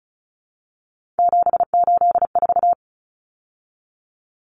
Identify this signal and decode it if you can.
Morse code, heard as a keyed tone, about 34 wpm, 700 Hz. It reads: 784
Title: Numbers - 3-Digit 35wpm